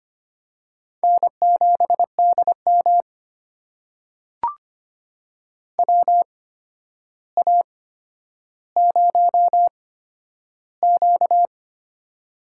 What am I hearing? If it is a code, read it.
Morse code, heard as a keyed tone, about 25 wpm, spaced as for 11 wpm, 700 Hz. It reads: N7DM WA0Q
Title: USA Call Signs - 4 Characters - 8x Character Spacing - 25wpm